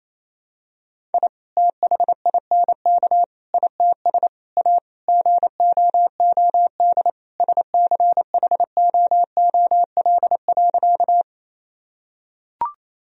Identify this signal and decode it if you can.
Morse code, heard as a keyed tone, about 28 wpm, 700 Hz. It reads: I THINK ITS A GOOD SCHOOL.
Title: Sentences from Top 1000 Words - Review Part 1 - 28wpm